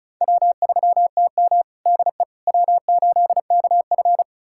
Morse code, heard as a keyed tone, about 34 wpm, 700 Hz. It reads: W3TM DE W8KF
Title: USA Callsign DE Callsign - 4 Characters - 35wpm